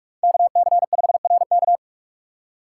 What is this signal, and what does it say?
Morse code, heard as a keyed tone, about 41 wpm, 700 Hz. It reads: KC5RK
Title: USA Call Signs - 5 Characters - 4x Character Spacing - 45wpm